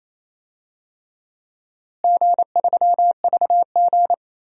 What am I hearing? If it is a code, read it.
Morse code, heard as a keyed tone, about 28 wpm, 700 Hz. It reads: G3VG
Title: DXCC-100 Call Signs - 4 Characters - Repeated 3x - 28wpm